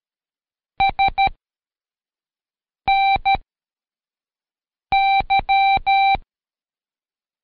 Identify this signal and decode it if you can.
Morse code, heard as a keyed tone, about 13 wpm, 780 Hz. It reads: S N Y
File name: Code Quick / AudioDownld2 / Seg104-Day12-FiinalExam 1